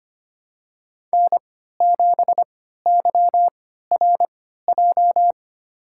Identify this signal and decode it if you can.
Morse code, heard as a keyed tone, about 25 wpm, spaced as for 14 wpm, 700 Hz. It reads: N7YRJ
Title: USA Call Signs - 5 Characters - 3x Character Spacing - 25wpm